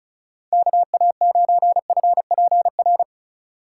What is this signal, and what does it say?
Morse code, heard as a keyed tone, about 36 wpm, 700 Hz. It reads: KA9FPR
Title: USA Call Signs - 6 Characters - 2x Character Spacing - 35wpm